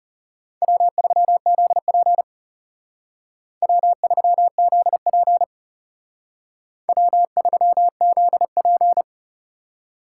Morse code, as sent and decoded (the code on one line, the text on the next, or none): .-- ...-- --.. .--. / .-- ...-- --.. .--. / .-- ...-- --.. .--.
W3ZP W3ZP W3ZP